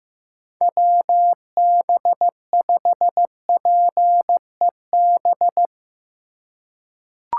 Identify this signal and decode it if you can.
Morse code, heard as a keyed tone, about 15 wpm, 700 Hz. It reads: WB5PEB
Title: USA Call Signs - 6 Characters - 2x Character Spacing - 15wpm